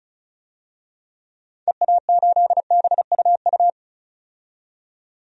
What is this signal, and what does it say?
Morse code, heard as a keyed tone, about 36 wpm, 700 Hz. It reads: EA8BUU
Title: DXCC-100 Call Signs - 6 Characters - 35wpm